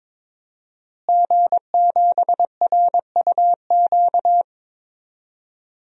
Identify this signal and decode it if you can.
Morse code, heard as a keyed tone, about 22 wpm, 700 Hz. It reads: G7RUQ
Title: DXCC-100 Call Signs - 5 Characters - Repeated 2x - 22wpm